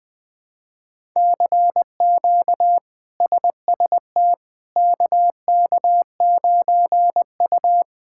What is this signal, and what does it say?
Morse code, heard as a keyed tone, about 20 wpm, 700 Hz. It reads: CQ SST KK9U
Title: K1USN SST QSO Practice - 20wpm